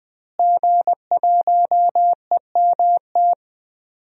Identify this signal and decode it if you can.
Morse code, heard as a keyed tone, about 20 wpm, 700 Hz. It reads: G1EMT
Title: DXCC-100 Call Signs - 5 Characters - Repeated 2x - 20wpm